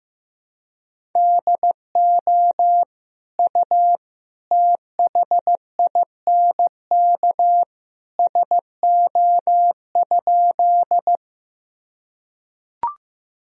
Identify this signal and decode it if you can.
Morse code, heard as a keyed tone, about 15 wpm, 700 Hz. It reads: DO U THINK SO?
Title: Sentences from Top 200 Words and Abbreviations 15wpm